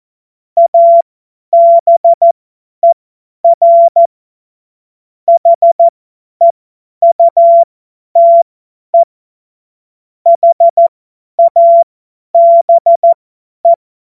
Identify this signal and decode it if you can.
Morse code, heard as a keyed tone, about 14 wpm, spaced as for 10 wpm, 680 Hz. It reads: ABER HEUTE HABE
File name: Morsekurs / Witze / Witz 09. Februar 2021, 50 F70